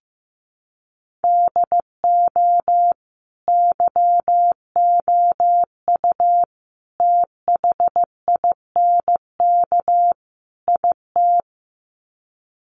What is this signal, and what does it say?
Morse code, heard as a keyed tone, about 15 wpm, 700 Hz. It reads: DO YOU THINK IT
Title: Sets of 4 Words 15wpm